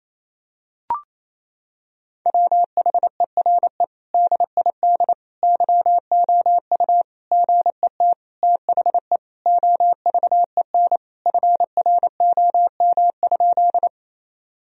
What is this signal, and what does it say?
Morse code, heard as a keyed tone, about 28 wpm, 700 Hz. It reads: WHERE DID YOU GET THE OVEN FROM?
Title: Sentences from Top 1000 Words - Review Part 4 - 28wpm